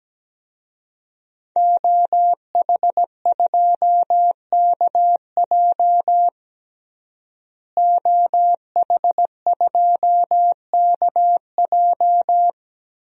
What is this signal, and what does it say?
Morse code, heard as a keyed tone, about 17 wpm, 700 Hz. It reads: OH2KJ OH2KJ
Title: DXCC-100 Call Signs - 5 Characters - Repeated 2x - 17wpm